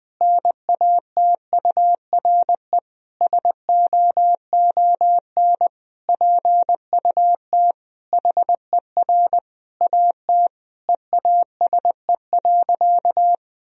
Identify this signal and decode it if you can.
Morse code, heard as a keyed tone, about 21 wpm, 700 Hz. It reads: NATURE SOON PUT HER AT EASE.
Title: Sentences from Top 1000 Words - Review Part 1 - 20wpm